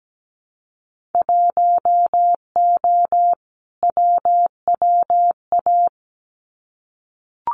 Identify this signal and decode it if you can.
Morse code, heard as a keyed tone, about 17 wpm, 700 Hz. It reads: 1O WWA
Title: Winter Field Day Exchanges 17wpm